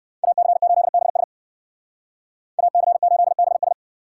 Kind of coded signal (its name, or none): Morse code